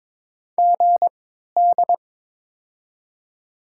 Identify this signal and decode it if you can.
Morse code, heard as a keyed tone, about 22 wpm, spaced as for 13 wpm, 700 Hz. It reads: GD
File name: Top 90 QSO Elements - 3x Character Spacing - 22wpm